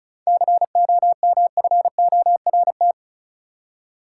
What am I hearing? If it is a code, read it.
Morse code, heard as a keyed tone, about 34 wpm, 700 Hz. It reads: COMFORT
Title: Seven Letter Words - 35wpm